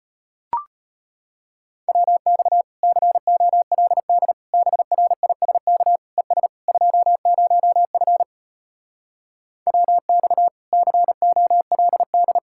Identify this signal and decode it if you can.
Morse code, heard as a keyed tone, about 33 wpm, 700 Hz. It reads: WX COLD BRISK ES 20F WX COLD